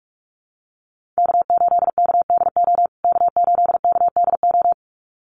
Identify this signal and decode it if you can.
Morse code, heard as a keyed tone, about 44 wpm, 700 Hz. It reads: K8KDO K8KDO
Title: USA Call Signs Repeated 2x 45wpm